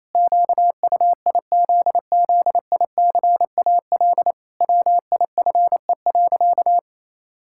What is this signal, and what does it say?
Morse code, heard as a keyed tone, about 28 wpm, 700 Hz. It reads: QUIZZICAL WIFE.
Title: Pangram Challenge v2 - 28wpm